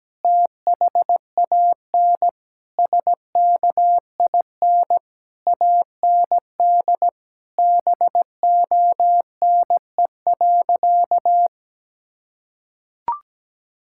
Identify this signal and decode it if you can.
Morse code, heard as a keyed tone, about 17 wpm, 700 Hz. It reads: THAN SKIN AND BONE.
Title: Sentences from Top 1600 Words 17wpm